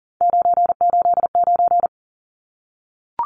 Morse code, heard as a keyed tone, about 40 wpm, 700 Hz. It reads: 989
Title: Numbers - 3-Digit 40wpm